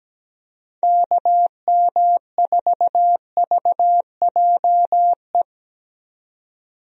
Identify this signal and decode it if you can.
Morse code, heard as a keyed tone, about 17 wpm, 700 Hz. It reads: KM4VJE
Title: USA Call Signs - 6 Characters - 17wpm